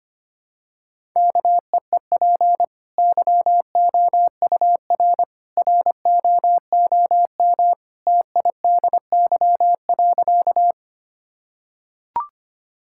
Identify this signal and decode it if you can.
Morse code, heard as a keyed tone, about 25 wpm, 700 Hz. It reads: KEEP YOUR ROOM TIDY.
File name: Easy Sentences with Hints - 25wpm